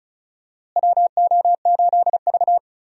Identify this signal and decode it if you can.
Morse code, heard as a keyed tone, about 34 wpm, 700 Hz. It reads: WO8V
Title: USA Call Signs - 4 Characters - Repeated 2x - 35wpm